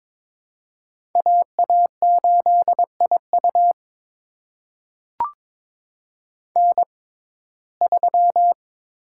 Morse code, heard as a keyed tone, about 22 wpm, spaced as for 16 wpm, 700 Hz. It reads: AA8IU N3
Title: USA Call Signs - 5 Characters - 6x Character Spacing - 22wpm